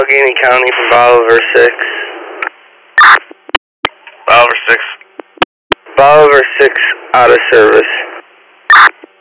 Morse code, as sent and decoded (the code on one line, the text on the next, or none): none